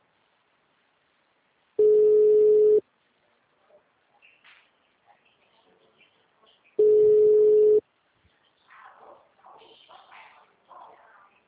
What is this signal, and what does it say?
Ringback tone (Europe, 425 Hz).